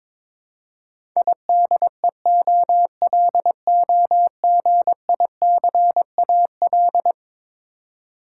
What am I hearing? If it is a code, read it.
Morse code, heard as a keyed tone, about 21 wpm, 700 Hz. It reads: IDEOLOGICAL
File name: practice / 22wpm / Eleven Letter Words - 22wpm